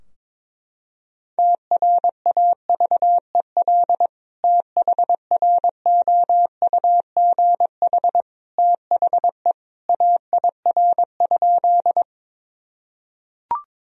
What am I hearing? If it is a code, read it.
Morse code, heard as a keyed tone, about 22 wpm, 700 Hz. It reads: TRAVEL THROUGH THE AIR?
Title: Sentences from Top 1000 Words - Review Part 2 - 22wpm